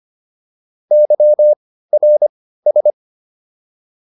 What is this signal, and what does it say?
Morse code, heard as a keyed tone, about 25 wpm, spaced as for 15 wpm, 600 Hz. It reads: YRS